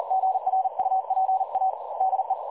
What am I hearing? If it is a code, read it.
Morse code, heard as a keyed tone, about 41 wpm, 750 Hz. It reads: GRUPA DE
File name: dlazaawansowanych / nagraniazpasma / websdr recording 3565.0kHz